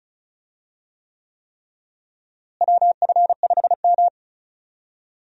Morse code, heard as a keyed tone, about 36 wpm, 700 Hz. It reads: WF5M
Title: USA Call Signs - 4 Characters - Repeated 3x - 35wpm